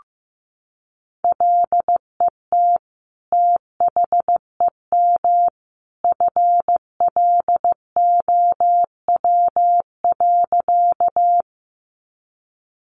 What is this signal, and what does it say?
Morse code, heard as a keyed tone, about 15 wpm, 700 Hz. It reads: LET THEM FLOW.
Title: Sentences from Top 1200 Words 15wpm